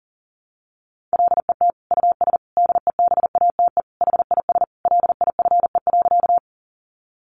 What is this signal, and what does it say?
Morse code, heard as a keyed tone, about 40 wpm, 700 Hz. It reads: LET US DEBATE HIS LIFE.
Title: Sentences from Top 1100 Words 40wpm